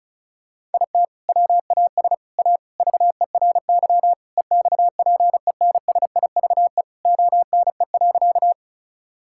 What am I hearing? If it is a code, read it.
Morse code, heard as a keyed tone, about 34 wpm, 690 Hz. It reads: IT WAS A VERY EXPENSIVE ONE.
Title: Sentences from Top 1000 Words - Review Part 6 - 35wpm